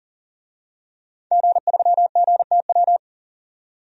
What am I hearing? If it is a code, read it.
Morse code, heard as a keyed tone, about 40 wpm, 700 Hz. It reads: G3GTW
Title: DXCC-100 Call Signs - 5 Characters - 40wpm